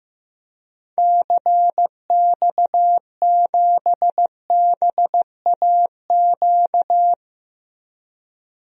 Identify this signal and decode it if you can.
Morse code, heard as a keyed tone, about 15 wpm, 700 Hz. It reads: CX7BAQ
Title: DXCC-100 Call Signs - 6 Characters - Repeated 2x - 15wpm